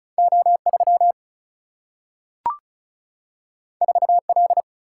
Morse code, heard as a keyed tone, about 36 wpm, 700 Hz. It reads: O 3 4 L
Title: Single-Letter Single-Number 35wpm